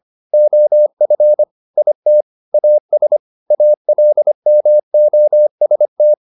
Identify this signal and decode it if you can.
Morse code, heard as a keyed tone, about 25 wpm, 600 Hz. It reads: OF IT AS ALMOST
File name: the old man and the sea part 1 25wpm